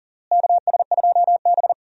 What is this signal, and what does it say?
Morse code, heard as a keyed tone, about 40 wpm, 700 Hz. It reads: KS2B
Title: USA Call Signs - 4 Characters - Repeated 3x - 40wpm